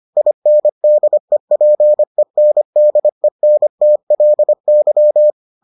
Morse code, heard as a keyed tone, about 25 wpm, 600 Hz. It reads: INDEPENDENTLY